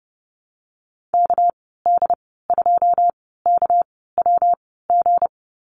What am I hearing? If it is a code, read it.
Morse code, heard as a keyed tone, about 30 wpm, spaced as for 17 wpm, 700 Hz. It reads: KD2KWG
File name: Lesson 37g - Letter x - Call Signs 30wpm